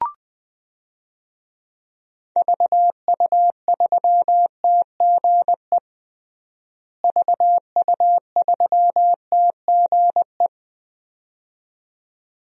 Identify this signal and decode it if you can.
Morse code, heard as a keyed tone, about 20 wpm, 700 Hz. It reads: VU3TGE VU3TGE
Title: DXCC-100 Call Signs - 6 Characters - Repeated 2x - 20wpm